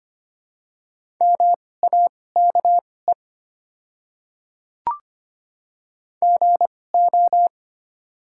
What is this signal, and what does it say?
Morse code, heard as a keyed tone, about 25 wpm, spaced as for 18 wpm, 700 Hz. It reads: MAKE GO
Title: 2x Character Spacing - Top 300 Words - 25wpm